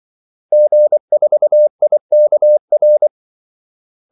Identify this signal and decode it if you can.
Morse code, heard as a keyed tone, about 24 wpm, 600 Hz. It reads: G4IKR
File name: Callsigns-24wpm